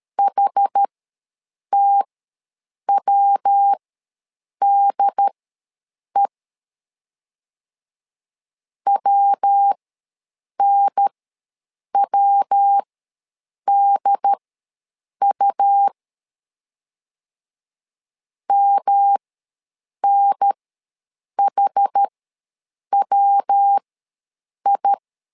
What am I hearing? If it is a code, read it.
Morse code, heard as a keyed tone, about 13 wpm, spaced as for 7 wpm, 800 Hz. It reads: HTWDE WNWDU MNHWI
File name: LEZIONI / 3 MWRO / 3 MW ws 12wpm 1816